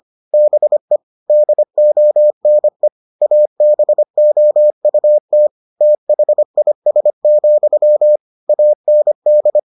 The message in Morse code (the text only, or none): BE DONE ABOUT THIS, AND